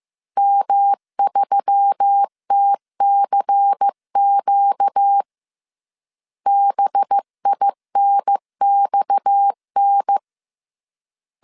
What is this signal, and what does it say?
Morse code, heard as a keyed tone, about 15 wpm, 800 Hz. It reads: M3TCQ BINXN